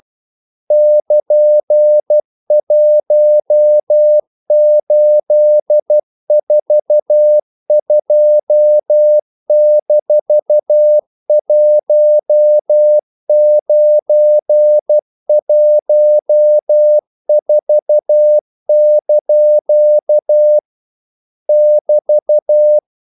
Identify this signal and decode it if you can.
Morse code, heard as a keyed tone, about 12 wpm, 600 Hz. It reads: <KN>1842-1914) <BT>
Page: English quotations in CW (telegraphy - morse)